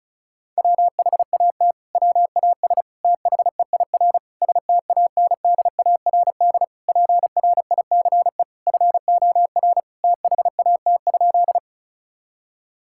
Morse code, heard as a keyed tone, about 36 wpm, 700 Hz. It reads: WHAT WAS THEIR STANDARD PRICE FOR THAT?
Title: Sentences from Top 1000 Words - Review Part 6 - 35wpm